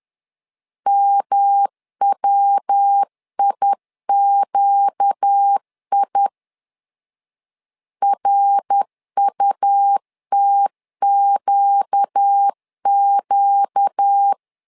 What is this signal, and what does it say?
Morse code, heard as a keyed tone, about 10 wpm, 800 Hz. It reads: MWIQI RUTQQ
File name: LEZIONI / 4 Q 10wpm 2210